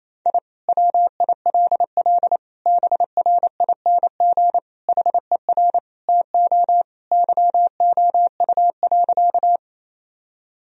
Morse code, heard as a keyed tone, about 28 wpm, 700 Hz. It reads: I WILL BRING HER TO YOU.
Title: Sentences from Top 1000 Words - Review Part 3 - 28wpm